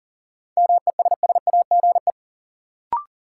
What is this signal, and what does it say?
Morse code, heard as a keyed tone, about 40 wpm, 700 Hz. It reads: MESSAGE
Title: Sentences from Top 1100 Words - New Word Drill Alt - 40wpm